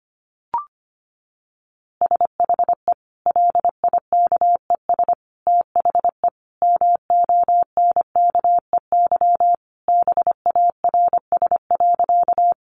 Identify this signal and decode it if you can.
Morse code, heard as a keyed tone, about 25 wpm, 700 Hz. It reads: SHE LIKES THE MONKEY BARS.
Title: Easy Phrases 25wpm